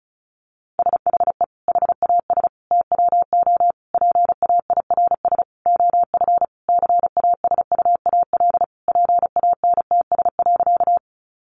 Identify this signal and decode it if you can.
Morse code, heard as a keyed tone, about 34 wpm, 700 Hz. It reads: SHE HAS TWO PAIRS OF CASUAL PANTS.
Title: Additional Easy Phrases 35wpm